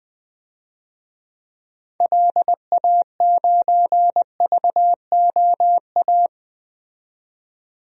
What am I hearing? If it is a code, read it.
Morse code, heard as a keyed tone, about 20 wpm, 700 Hz. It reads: LA9VOA
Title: DXCC-100 Call Signs - 6 Characters - 20wpm